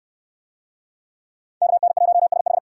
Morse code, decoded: KN1SH

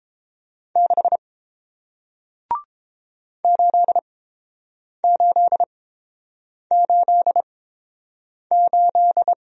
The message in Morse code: 6 8888

